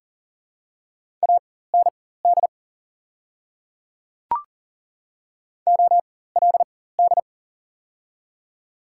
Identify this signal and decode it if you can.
Morse code, heard as a keyed tone, about 40 wpm, spaced as for 19 wpm, 700 Hz. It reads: AND OLD
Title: 4x Character Spacing - Top 200 Words - 40wpm